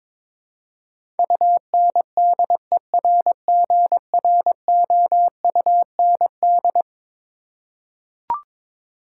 Morse code, heard as a keyed tone, about 21 wpm, 700 Hz. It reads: UNDERGROUND
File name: practice / 22wpm / Eleven Letter Words - 22wpm